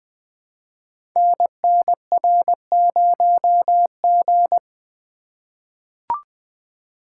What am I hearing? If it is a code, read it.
Morse code, heard as a keyed tone, about 20 wpm, 700 Hz. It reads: NNR0G
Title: Code Groups - 5 Letter:Numbers - 20wpm